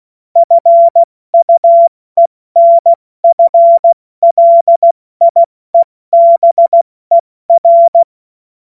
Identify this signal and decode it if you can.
Morse code, heard as a keyed tone, about 16 wpm, 680 Hz. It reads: FUENF LIEBER
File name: Morsekurs / Witze / Witz 17. August 2021, 70 F80